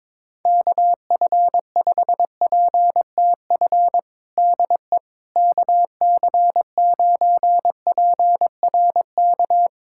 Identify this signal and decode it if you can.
Morse code, heard as a keyed tone, about 22 wpm, 700 Hz. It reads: KF5PTF DE KC9PRK
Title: USA Callsign DE Callsign - 6 Characters - 22wpm